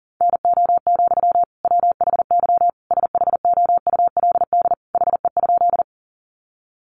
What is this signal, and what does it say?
Morse code, heard as a keyed tone, about 40 wpm, 700 Hz. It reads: NO, WHY SHOULD HE?
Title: Sentences from Top 300 Words 40wpm